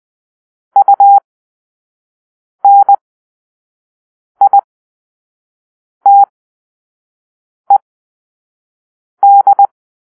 Morse code, heard as a keyed tone, about 20 wpm, spaced as for 5 wpm, 800 Hz. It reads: UNITED
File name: Grp-3 Drill-1